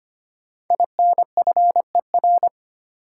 Morse code, heard as a keyed tone, about 25 wpm, 700 Hz. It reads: INFER
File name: 5 Letter Words - First 3 Letters Given - 25wpm